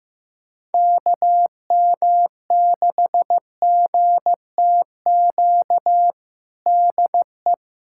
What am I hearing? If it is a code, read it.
Morse code, heard as a keyed tone, about 16 wpm, 700 Hz. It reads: KM6GTQ DE
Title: USA Callsign DE Callsign - 6 Characters - 15wpm